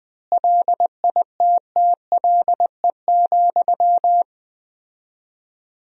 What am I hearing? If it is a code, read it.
Morse code, heard as a keyed tone, about 20 wpm, 700 Hz. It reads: LITTLE,
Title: 4x Word-Spacing - Sentences from Top 400 Words - 20wpm